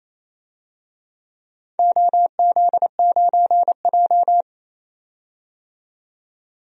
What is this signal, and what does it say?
Morse code, heard as a keyed tone, about 28 wpm, 700 Hz. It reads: OZ9J
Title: DXCC-100 Call Signs - 4 Characters - 28wpm